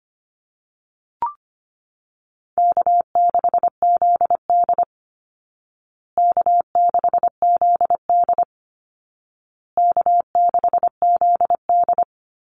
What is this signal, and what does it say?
Morse code, heard as a keyed tone, about 25 wpm, 700 Hz. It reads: K6ZD K6ZD K6ZD
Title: Easy Call Signs Repeated 3x - 25wpm